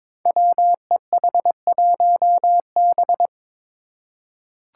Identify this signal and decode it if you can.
Morse code, heard as a keyed tone, about 22 wpm, 700 Hz. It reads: WEH1B
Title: Five Challenging Letter-Number Combinations 22wpm